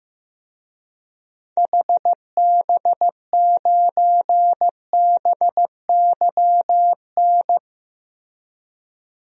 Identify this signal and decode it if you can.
Morse code, heard as a keyed tone, about 15 wpm, 700 Hz. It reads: HB9BYN
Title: DXCC-100 Call Signs - 6 Characters - Repeated 2x - 15wpm